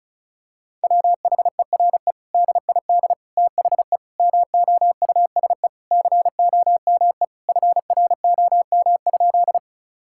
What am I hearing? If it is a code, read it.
Morse code, heard as a keyed tone, about 36 wpm, 700 Hz. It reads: WHERE DID THE MOUSE COME FROM?
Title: Sentences from Top 1000 Words - Review Part 5 - 35wpm